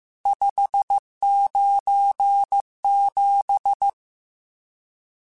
Morse code, heard as a keyed tone, about 15 wpm, 800 Hz. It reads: 597
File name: RSTs / RST-mp3s / Various RST @13